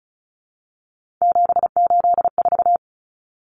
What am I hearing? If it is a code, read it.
Morse code, heard as a keyed tone, about 34 wpm, 700 Hz. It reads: 784
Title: Numbers - 3-Digit 35wpm